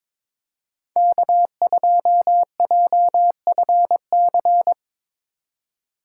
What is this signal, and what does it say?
Morse code, heard as a keyed tone, about 22 wpm, 700 Hz. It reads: K2JFC
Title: USA Call Signs - 5 Characters - 4x Character Spacing - 22wpm